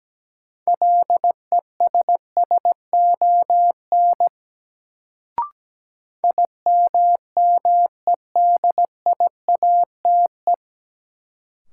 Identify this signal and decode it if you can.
Morse code, heard as a keyed tone, about 17 wpm, 700 Hz. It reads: LESSON IMMEDIATE
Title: Sentences from Top 1200 Words - New Word Drill Alt - 17wpm